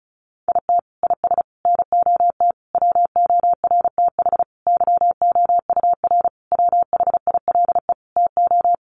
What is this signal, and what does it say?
Morse code, heard as a keyed tone, about 34 wpm, 700 Hz. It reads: IT IS NOT WORTH YOUR WHILE TO
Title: Sentences from Top 800 Words 35wpm